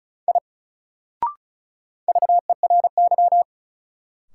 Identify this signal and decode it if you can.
Morse code, heard as a keyed tone, about 36 wpm, 700 Hz. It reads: I VERY
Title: Sentences from Top 200 Words - New Word Drill - 35wpm